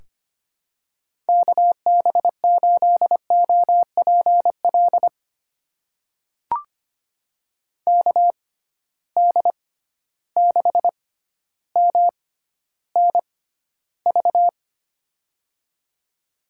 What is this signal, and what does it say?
Morse code, heard as a keyed tone, about 25 wpm, spaced as for 13 wpm, 700 Hz. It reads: KB8OPL KD6MNV
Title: USA Call Signs - 6 Characters - 6x Character Spacing - 25wpm